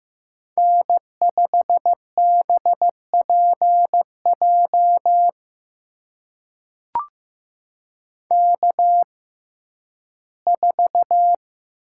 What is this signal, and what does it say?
Morse code, heard as a keyed tone, about 15 wpm, spaced as for 11 wpm, 700 Hz. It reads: N5BPJ K4
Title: USA Call Signs - 5 Characters - 6x Character Spacing - 15wpm